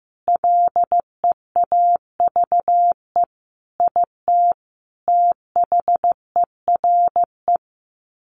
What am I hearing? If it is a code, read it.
Morse code, heard as a keyed tone, about 15 wpm, 700 Hz. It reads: LEAVE IT THERE
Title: Sets of 5 Words Encore2 15wpm